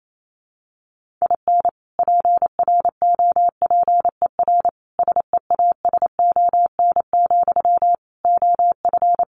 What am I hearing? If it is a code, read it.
Morse code, heard as a keyed tone, about 28 wpm, 700 Hz. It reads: IN PROPER SEASON, OF
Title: Sentences from Top 1400 Words 28wpm